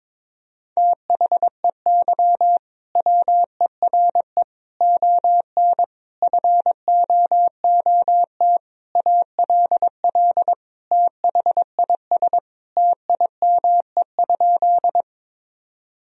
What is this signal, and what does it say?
Morse code, heard as a keyed tone, about 22 wpm, 700 Hz. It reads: THEY WERE ON FOOT ALL THIS TIME?
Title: Sentences from Top 1000 Words - Review Part 3 - 22wpm